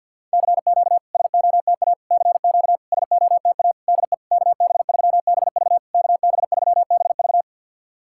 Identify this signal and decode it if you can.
Morse code, heard as a keyed tone, about 51 wpm, 690 Hz. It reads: CQ SOTA CQ SOTA DE KB3BV KB3BV
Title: SOTA QSO Practice 50wpm